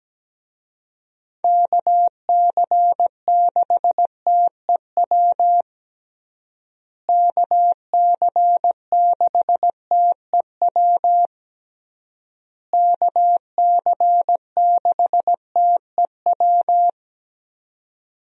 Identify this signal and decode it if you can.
Morse code, heard as a keyed tone, about 17 wpm, 700 Hz. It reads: KC6TEW KC6TEW KC6TEW
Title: USA Call Signs - 6 Characters - Repeated 3x - 17wpm